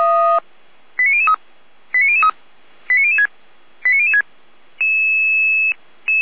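Five-tone selective call (ZVEI), heard as quick runs of five tones, tones of about 70 ms. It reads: 89003 89003 89006 89006
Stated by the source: ZVEI five-tone call 89010 received on an SDR by feuerwehr-orlamuende.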